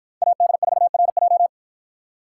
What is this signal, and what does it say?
Morse code, heard as a keyed tone, about 52 wpm, 700 Hz. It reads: AD4RJ